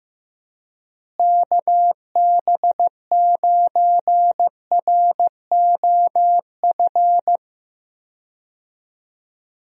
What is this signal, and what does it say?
Morse code, heard as a keyed tone, about 16 wpm, 700 Hz. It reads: KB9ROF
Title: USA Call Signs - 6 Characters - Repeated 3x - 15wpm